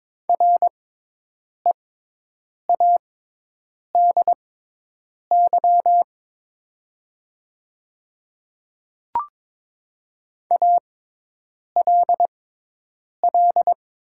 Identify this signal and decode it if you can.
Morse code, heard as a keyed tone, about 22 wpm, spaced as for 8 wpm, 700 Hz. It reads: READY ALL
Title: 6x Character Spacing - Top 500 Words - 22wpm